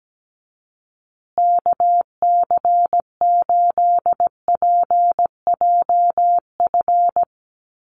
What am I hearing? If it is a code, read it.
Morse code, heard as a keyed tone, about 17 wpm, 700 Hz. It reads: KC8PJF
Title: USA Call Signs Repeated 3x 17wpm